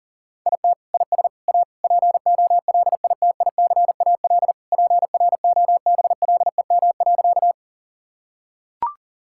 Morse code, heard as a keyed tone, about 40 wpm, 700 Hz. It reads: IT IS A POLITICAL PROBLEM.
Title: Sentences from Top 1000 Words - Review Part 5 - 40wpm